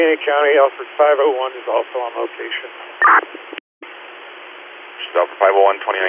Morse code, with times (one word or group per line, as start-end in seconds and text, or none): none